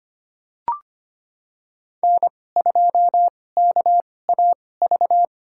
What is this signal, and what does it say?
Morse code, heard as a keyed tone, about 25 wpm, spaced as for 18 wpm, 700 Hz. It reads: N2KAV
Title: USA Call Signs - 5 Characters - 2x Character Spacing - 25wpm